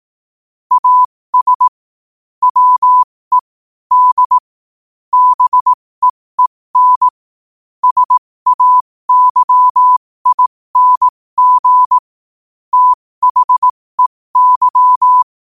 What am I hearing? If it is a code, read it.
Morse code, heard as a keyed tone, about 18 wpm, spaced as for 15 wpm, 1000 Hz. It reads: AS WE D BEEN SAYING THEY
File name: qotdfiles / 15wpm